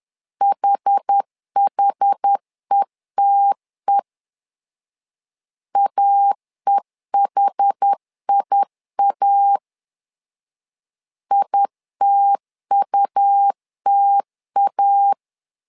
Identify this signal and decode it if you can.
Morse code, heard as a keyed tone, about 10 wpm, 800 Hz. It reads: HHETE AEHIA ITUTA
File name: LEZIONI / 2 UHDN / 2 UH 10wpm 1714